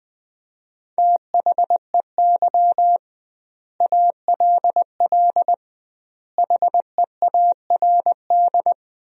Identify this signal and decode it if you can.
Morse code, heard as a keyed tone, about 21 wpm, 700 Hz. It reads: THEY ALL HEARD
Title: 2x Word-Spacing - Sentences from Top 400 Words - 20wpm